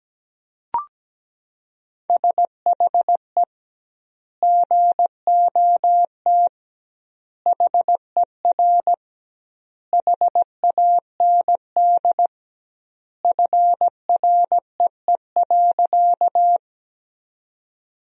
Morse code, heard as a keyed tone, about 17 wpm, 700 Hz. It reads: SHE GOT HER HAND FREE.
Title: 2x Word-Spacing - Sentences from Top 500 Words - 17wpm